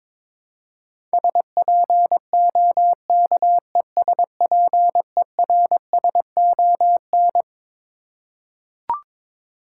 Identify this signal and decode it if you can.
Morse code, heard as a keyed tone, about 22 wpm, 700 Hz. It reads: SPOKESPERSON
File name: Twelve Letter Words - 22wpm